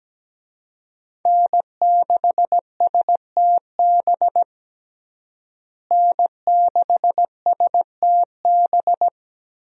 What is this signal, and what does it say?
Morse code, heard as a keyed tone, about 17 wpm, 700 Hz. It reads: N6STB N6STB
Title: USA Call Signs - 5 Characters - Repeated 2x - 17wpm